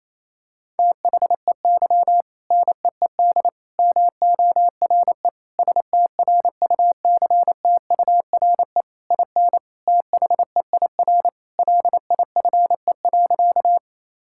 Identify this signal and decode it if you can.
Morse code, heard as a keyed tone, about 28 wpm, 700 Hz. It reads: THEY NEED MORE STRUCTURE IN THEIR LIFE.
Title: Sentences from Top 1000 Words - Review Part 6 - 28wpm